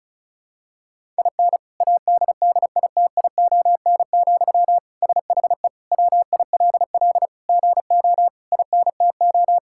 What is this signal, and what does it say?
Morse code, heard as a keyed tone, about 34 wpm, 700 Hz. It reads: IN ADDITION, SHE WILL GO INTO
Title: Sentences from Top 1000 Words - Review Part 6 - 35wpm